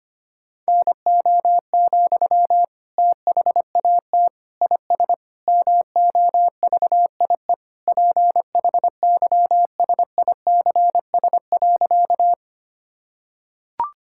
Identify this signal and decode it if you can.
Morse code, heard as a keyed tone, about 25 wpm, 700 Hz. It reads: NO, THAT IS MOVIE PHYSICS.
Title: Sentences from Top 1000 Words - Review Part 5 - 25wpm